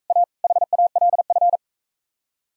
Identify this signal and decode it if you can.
Morse code, heard as a keyed tone, about 42 wpm, 700 Hz. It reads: A HALF